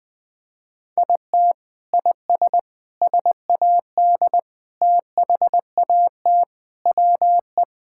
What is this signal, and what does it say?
Morse code, heard as a keyed tone, about 20 wpm, 700 Hz. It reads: IT IS SAD THAT WE
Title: Sentences with 4-Word Phrases - Spoken Beginning - 20wpm